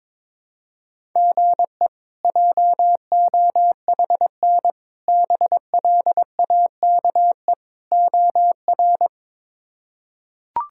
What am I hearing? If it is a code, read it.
Morse code, heard as a keyed tone, about 22 wpm, 700 Hz. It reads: GE JOHN BLAKE OR
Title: K1USN SST QSO Practice - 22wpm